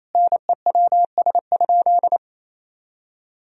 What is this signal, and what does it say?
Morse code, heard as a keyed tone, about 28 wpm, 700 Hz. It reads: NEWS?
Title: Sentences from Top 1000 Words - Review Part 4 - 28wpm